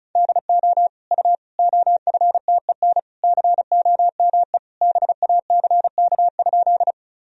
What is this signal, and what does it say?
Morse code, heard as a keyed tone, about 36 wpm, 700 Hz. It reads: DO U OFTEN COME BACK?
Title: Sentences from Top 300 Words and Abbreviations 35wpm